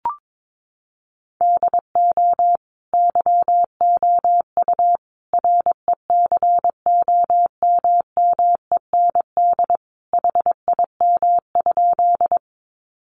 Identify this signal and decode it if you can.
Morse code, heard as a keyed tone, about 22 wpm, 700 Hz. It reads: DO YOU RECOMMEND HIM?
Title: Sentences from Top 1200 Words 22wpm